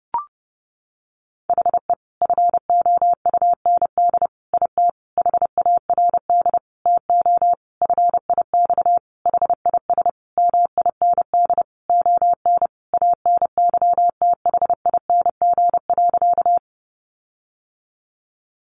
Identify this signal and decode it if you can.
Morse code, heard as a keyed tone, about 30 wpm, 700 Hz. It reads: HE FOUND IT HARD TO FIX HIS MIND ON ANYTHING.
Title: Sentences from Top 1400 Words 30wpm